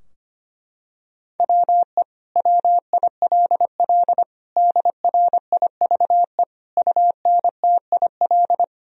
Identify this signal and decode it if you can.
Morse code, heard as a keyed tone, about 25 wpm, 700 Hz. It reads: WE WILL DRIVE UNTIL
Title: Sentences from Top 1000 Words - Review Part 2 - 25wpm